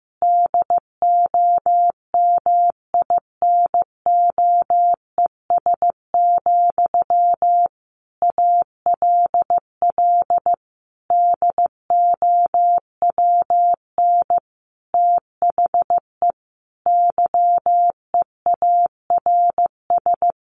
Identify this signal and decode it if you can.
Morse code, heard as a keyed tone, about 15 wpm, 700 Hz. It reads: DOMINOES, ALL DOWN THE YEARS
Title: A Sound Of Thunder 15wpm